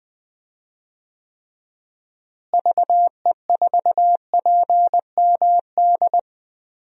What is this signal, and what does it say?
Morse code, heard as a keyed tone, about 20 wpm, 700 Hz. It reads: VE4PMD
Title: DXCC-100 Call Signs - 6 Characters - Repeated 3x - 20wpm